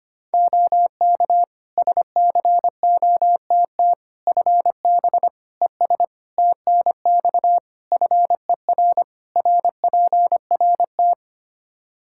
Morse code, heard as a keyed tone, about 25 wpm, 700 Hz. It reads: OK SCOTT FB ES TNX FER RPRT
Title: Practice QSO - Exchange 2 - Info - 25wpm